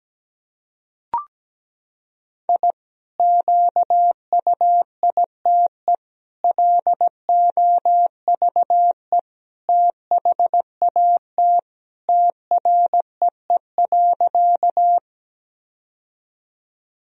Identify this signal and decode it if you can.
Morse code, heard as a keyed tone, about 17 wpm, 700 Hz. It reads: I QUITE LOVE THAT TREE.
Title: Sentences from Top 1000 Words - Review Part 1 - 17wpm